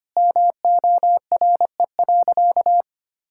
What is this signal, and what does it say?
Morse code, heard as a keyed tone, about 25 wpm, 700 Hz. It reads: MORE.
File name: practice / 25wpm / Sentences with 3-Word Phrases - 25wpm